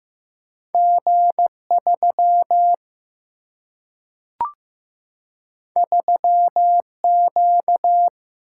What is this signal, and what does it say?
Morse code, heard as a keyed tone, about 15 wpm, 700 Hz. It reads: G 3 3 Q
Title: Single-Letter Single-Number 15wpm